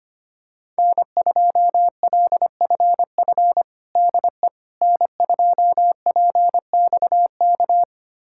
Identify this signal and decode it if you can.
Morse code, heard as a keyed tone, about 25 wpm, 700 Hz. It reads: N2LFF DE N2PXK
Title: USA Callsign DE Callsign - 5 Characters - 25wpm